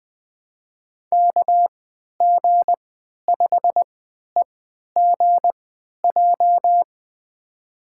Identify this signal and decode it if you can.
Morse code, heard as a keyed tone, about 20 wpm, spaced as for 11 wpm, 700 Hz. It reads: KG5EGJ
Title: USA Call Signs - 3x Character Spacing - 20wpm